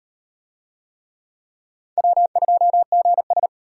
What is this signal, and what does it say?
Morse code, heard as a keyed tone, about 35 wpm, 700 Hz. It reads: W2GS